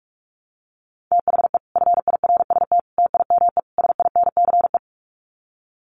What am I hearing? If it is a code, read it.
Morse code, heard as a keyed tone, about 44 wpm, 700 Hz. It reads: THE FIRST TIME SINCE
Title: Sets of 4 Words 45wpm